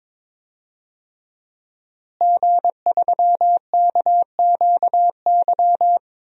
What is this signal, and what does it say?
Morse code, heard as a keyed tone, about 21 wpm, 700 Hz. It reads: G3KQY
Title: DXCC-100 Call Signs - 5 Characters - Repeated 3x - 22wpm